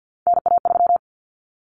Morse code, heard as a keyed tone, about 50 wpm, 700 Hz. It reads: NA3
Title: Common Cut Numbers - 3-Digit 50wpm